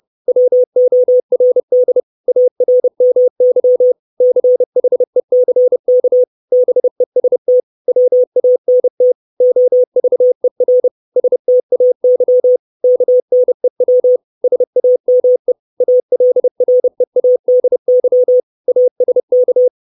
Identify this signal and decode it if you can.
Morse code, heard as a keyed tone, about 30 wpm, 500 Hz. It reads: WORD ARMY CHECK BEST WANT OVER STAY KNEW SAME ALREADY ASK
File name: 632 Words 30WPM